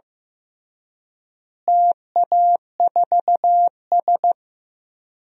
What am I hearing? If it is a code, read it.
Morse code, heard as a keyed tone, about 15 wpm, 700 Hz. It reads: TA4S
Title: DXCC-100 Call Signs - 4 Characters - Repeated 3x - 15wpm